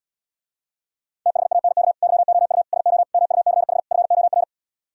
Morse code, heard as a keyed tone, about 78 wpm, 700 Hz. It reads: THANK YOU SO MUCH FOR